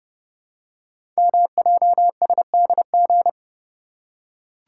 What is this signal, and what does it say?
Morse code, heard as a keyed tone, about 30 wpm, 700 Hz. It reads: MJSDG
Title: Code Groups - 5 Letters - 30wpm